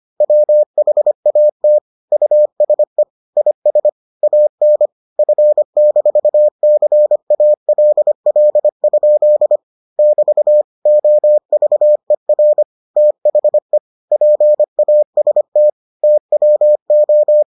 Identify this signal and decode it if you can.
Morse code, heard as a keyed tone, about 25 wpm, 600 Hz. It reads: WHAT USE IS AN F-CALL? <BT> OVER THE PAST TWO